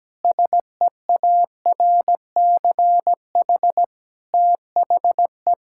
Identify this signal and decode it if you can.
Morse code, heard as a keyed tone, about 17 wpm, 700 Hz. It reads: SEARCH THE
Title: Sentences from Top 1000 Words - Review Part 5 - 17wpm